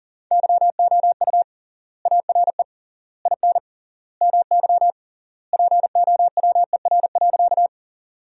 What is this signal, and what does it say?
Morse code, heard as a keyed tone, about 40 wpm, 700 Hz. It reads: YOU ARE IN MY POWER.
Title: 3x Word-Spacing - Sentences from Top 500 Words - 40wpm